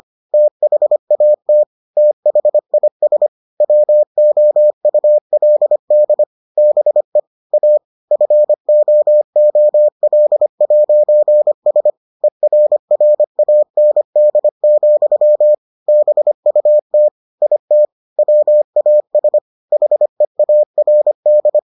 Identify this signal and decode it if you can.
Morse code, heard as a keyed tone, about 25 wpm, 600 Hz. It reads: THAT THIS WOULD BE A FOOL'S ERRAND, BUT IT WAS HEARD